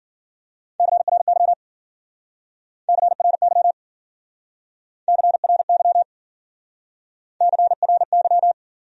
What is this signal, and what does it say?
Morse code, heard as a keyed tone, about 54 wpm, 700 Hz. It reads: CRY CRY CRY CRY